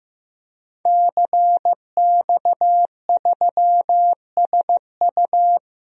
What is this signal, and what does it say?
Morse code, heard as a keyed tone, about 14 wpm, 700 Hz. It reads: CX3SU
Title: DXCC-100 Call Signs - 5 Characters - Repeated 3x - 15wpm